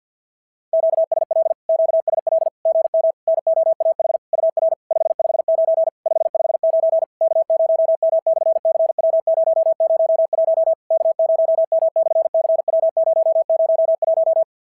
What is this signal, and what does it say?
Morse code, heard as a keyed tone, about 51 wpm, 690 Hz. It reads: QSL QSL GM NOAH UR 559 559 K0M/CW001 K0M/CW001
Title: SOTA QSO Practice 50wpm